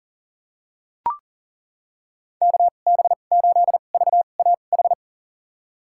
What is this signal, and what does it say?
Morse code, heard as a keyed tone, about 37 wpm, spaced as for 28 wpm, 700 Hz. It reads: KB8VAH
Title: USA Call Signs - 2x Character Spacing - 40wpm